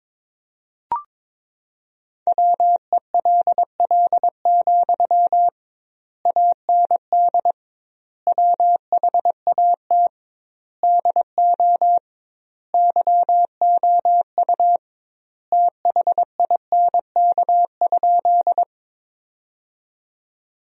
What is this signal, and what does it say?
Morse code, heard as a keyed tone, about 22 wpm, 700 Hz. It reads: WELL, AND WHAT DO YOU THINK?
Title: 2x Word-Spacing - Sentences from Top 200 Words - 22wpm